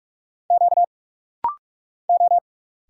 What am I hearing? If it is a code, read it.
Morse code, heard as a keyed tone, about 46 wpm, 700 Hz. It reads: Q O
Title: Single Letters - Rapid-Fire 45wpm